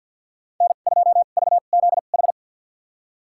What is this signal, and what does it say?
Morse code, heard as a keyed tone, about 49 wpm, spaced as for 36 wpm, 700 Hz. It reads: N2VZH
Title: USA Call Signs - 5 Characters - 2x Character Spacing - 50wpm